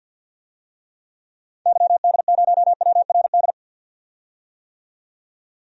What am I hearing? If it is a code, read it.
Morse code, heard as a keyed tone, about 52 wpm, 700 Hz. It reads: YD0WRD
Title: DXCC-100 Call Signs - 6 Characters - 50wpm